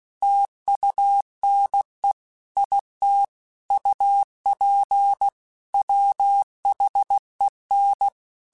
Morse code, heard as a keyed tone, about 16 wpm, 800 Hz. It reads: TUNE IT UP WHEN
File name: QSOs / QSOs-mp3s / WD0BGP 2@15